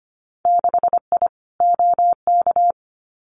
Morse code, heard as a keyed tone, about 25 wpm, 690 Hz. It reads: 6I OK
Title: Winter Field Day Exchanges Encore 25wpm